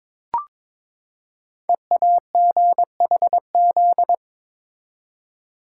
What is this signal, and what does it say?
Morse code, heard as a keyed tone, about 22 wpm, 700 Hz. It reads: EAGHZ
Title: Code Groups - 5 Letters - 22wpm